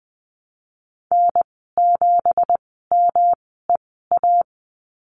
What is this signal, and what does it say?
Morse code, heard as a keyed tone, about 20 wpm, spaced as for 15 wpm, 700 Hz. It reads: N7MEA
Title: 167 Lesson 27g - Number 7 - Call Signs 20wpm